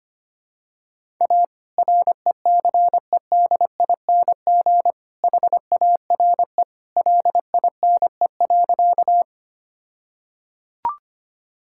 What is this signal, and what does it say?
Morse code, heard as a keyed tone, about 25 wpm, 700 Hz. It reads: A RECEDING HARE LINE.
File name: practice / 25wpm / All-Age Appropriate Jokes - 25wpm